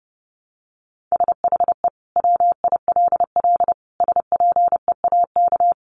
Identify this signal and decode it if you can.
Morse code, heard as a keyed tone, about 30 wpm, 700 Hz. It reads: SHE WILL SPEAK
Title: Sentences from Top 1100 Words 30wpm